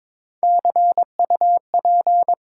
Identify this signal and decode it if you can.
Morse code, heard as a keyed tone, about 22 wpm, 700 Hz. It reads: CUP
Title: Sentences from Top 1600 Words - New Word Drill - 22wpm